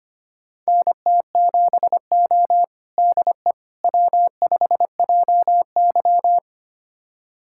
Morse code, heard as a keyed tone, about 25 wpm, 700 Hz. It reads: NT7O DE W5JY
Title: USA Callsign DE Callsign - 4 Characters - 25wpm